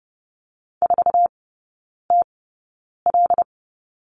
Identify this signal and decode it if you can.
Morse code, heard as a keyed tone, about 30 wpm, spaced as for 9 wpm, 700 Hz. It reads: 4TL
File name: 081 Lesson 14e - Number 2 - Three Characters 30wpm